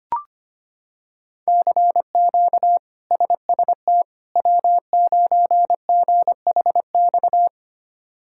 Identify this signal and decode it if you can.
Morse code, heard as a keyed tone, about 25 wpm, 700 Hz. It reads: CQ SST W9GHX
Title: K1USN SST QSO Practice - 25wpm